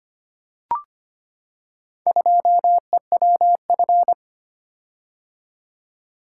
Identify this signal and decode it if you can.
Morse code, heard as a keyed tone, about 25 wpm, 700 Hz. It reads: 2EWF
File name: practice / 25wpm / Four Letter-Number Combinations - 25wpm